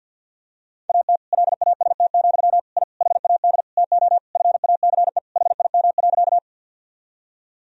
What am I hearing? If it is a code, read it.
Morse code, heard as a keyed tone, about 52 wpm, 700 Hz. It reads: AT LAST, I HAD TO FACE HIM.
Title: Sentences from Top 1000 Words - Review Part 2 - 50wpm